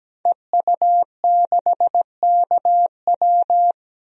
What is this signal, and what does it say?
Morse code, heard as a keyed tone, about 17 wpm, 700 Hz. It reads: EU6KW
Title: DXCC-100 Call Signs - 5 Characters - 17wpm